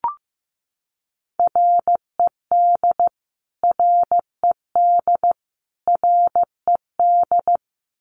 Morse code, heard as a keyed tone, about 15 wpm, 700 Hz. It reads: RED RED RED
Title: Top 500 Words Repeated 15wpm